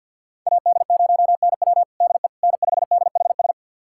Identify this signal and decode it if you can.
Morse code, heard as a keyed tone, about 45 wpm, 700 Hz. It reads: AD0NW DE N5DHS
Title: USA Callsign DE Callsign - 5 Characters - 50wpm